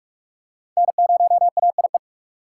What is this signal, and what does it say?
Morse code, heard as a keyed tone, about 44 wpm, 700 Hz. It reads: N0AIE
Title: K1USN SST QSO Practice - 45wpm